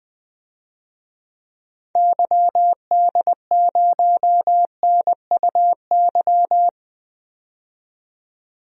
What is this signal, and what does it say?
Morse code, heard as a keyed tone, about 20 wpm, 700 Hz. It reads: YD0NUY
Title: DXCC-100 Call Signs - 6 Characters - 20wpm